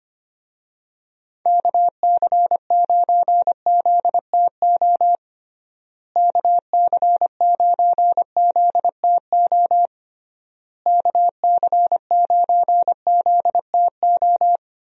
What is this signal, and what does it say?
Morse code, heard as a keyed tone, about 25 wpm, 700 Hz. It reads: KC9ZTO KC9ZTO KC9ZTO
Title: USA Call Signs - 6 Characters - Repeated 3x - 25wpm